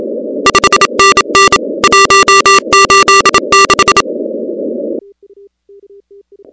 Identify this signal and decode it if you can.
Morse code, heard as a keyed tone, about 27 wpm, 1180 Hz, with noise. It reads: 5NN 186
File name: contests / 2019-cw-wpx-kt5j-k5pi- -2330 0-16